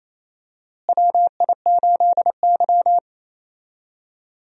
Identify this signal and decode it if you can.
Morse code, heard as a keyed tone, about 28 wpm, 700 Hz. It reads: WI8Y
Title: USA Call Signs - 4 Characters - Repeated 3x - 28wpm